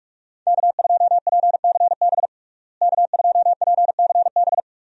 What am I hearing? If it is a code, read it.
Morse code, heard as a keyed tone, about 46 wpm, 700 Hz. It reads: K2PCB K2PCB
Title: USA Call Signs - 5 Characters - Repeated 2x - 45wpm